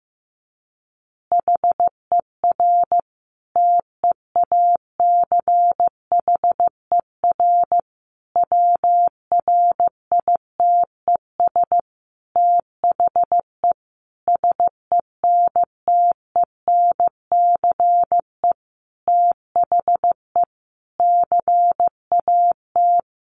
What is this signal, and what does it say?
Morse code, heard as a keyed tone, about 15 wpm, 700 Hz. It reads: HER TEACHER WRITES THE SENTENCE THE CAT
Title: Additional Easy Phrases 15wpm